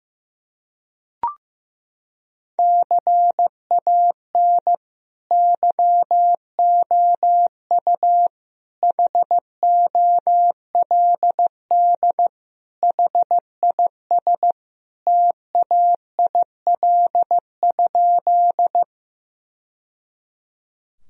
Morse code, heard as a keyed tone, about 16 wpm, 700 Hz. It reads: CAN YOU HOLD HIS TAIL?
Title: Sentences from Top 1000 Words - Review Part 3 - 15wpm